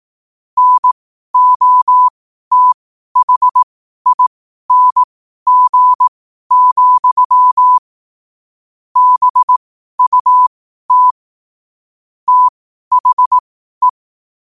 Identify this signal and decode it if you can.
Morse code, heard as a keyed tone, about 18 wpm, spaced as for 12 wpm, 1000 Hz. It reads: NOTHING, BUT THE
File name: qotdfiles / 12wpm